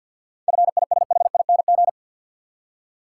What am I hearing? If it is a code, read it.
Morse code, heard as a keyed tone, about 51 wpm, 700 Hz. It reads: FISHING
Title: Seven Letter Words - 50wpm